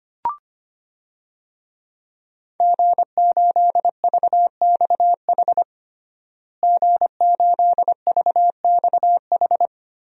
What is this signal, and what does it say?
Morse code, heard as a keyed tone, about 25 wpm, 700 Hz. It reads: G8VXH G8VXH
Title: DXCC-100 Call Signs - 5 Characters - Repeated 2x - 25wpm